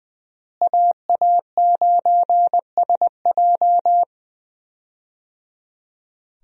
Morse code, heard as a keyed tone, about 21 wpm, 700 Hz. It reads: AA9SJ
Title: USA Call Signs - 5 Characters - Repeated 3x - 20wpm